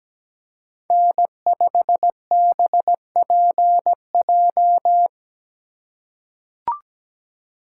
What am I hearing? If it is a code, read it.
Morse code, heard as a keyed tone, about 17 wpm, 700 Hz. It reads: N5BPJ
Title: USA Call Signs - 5 Characters - 4x Character Spacing - 17wpm